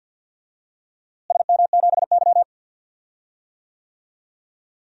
Morse code, heard as a keyed tone, about 52 wpm, 700 Hz. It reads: SM7Y
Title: DXCC-100 Call Signs - 4 Characters - Repeated 2x - 50wpm